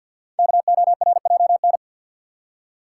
Morse code, heard as a keyed tone, about 48 wpm, 700 Hz. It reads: KORJN
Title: Code Groups - 5 Letters - 50wpm